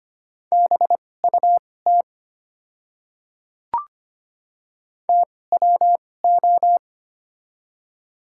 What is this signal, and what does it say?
Morse code, heard as a keyed tone, about 25 wpm, spaced as for 18 wpm, 700 Hz. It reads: BUT TWO
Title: 2x Character Spacing - Top 200 Words - 25wpm